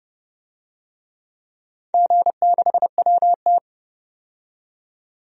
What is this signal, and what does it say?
Morse code, heard as a keyed tone, about 31 wpm, 700 Hz. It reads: G6WT
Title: DXCC-100 Call Signs - 4 Characters - 30wpm